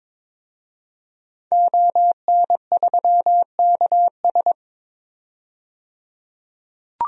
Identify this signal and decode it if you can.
Morse code, heard as a keyed tone, about 22 wpm, 700 Hz. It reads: ON3KS
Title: DXCC-100 Call Signs - 5 Characters - 22wpm